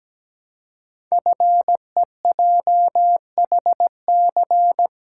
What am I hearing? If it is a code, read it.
Morse code, heard as a keyed tone, about 17 wpm, 700 Hz. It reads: FEJHC
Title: Code Groups - 5 Letters - 17wpm